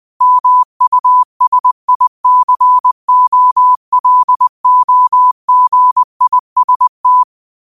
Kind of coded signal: Morse code